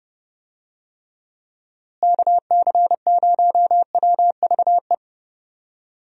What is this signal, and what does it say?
Morse code, heard as a keyed tone, about 30 wpm, 700 Hz. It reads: KC0WVE
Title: USA Call Signs - 6 Characters - Repeated 2x - 30wpm